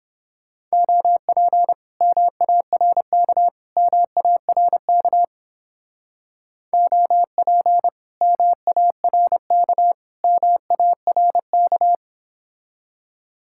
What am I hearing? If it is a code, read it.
Morse code, heard as a keyed tone, about 31 wpm, 700 Hz. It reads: OP MARK MARK OP MARK MARK